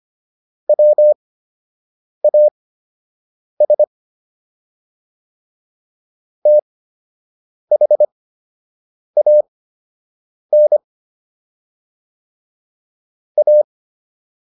Morse code, heard as a keyed tone, about 25 wpm, spaced as for 7 wpm, 600 Hz. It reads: WAS THAN A